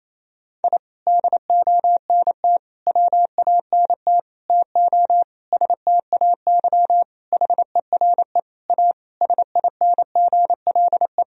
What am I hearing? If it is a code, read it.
Morse code, heard as a keyed tone, about 28 wpm, 700 Hz. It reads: I DONT WANT TO STAY HERE A SINGLE
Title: Sentences from Top 1000 Words - Review Part 4 - 28wpm